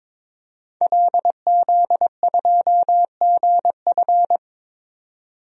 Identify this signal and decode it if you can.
Morse code, heard as a keyed tone, about 22 wpm, 700 Hz. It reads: LZ2GF
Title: DXCC-100 Call Signs - 5 Characters - 22wpm